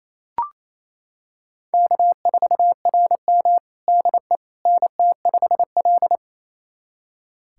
Morse code, heard as a keyed tone, about 29 wpm, 700 Hz. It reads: K4RM DE NT5L
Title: USA Callsign DE Callsign - 4 Characters - 28wpm